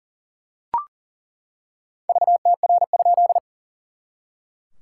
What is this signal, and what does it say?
Morse code, heard as a keyed tone, about 40 wpm, 690 Hz. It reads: VTR?
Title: SOTA QSO Practice 40wpm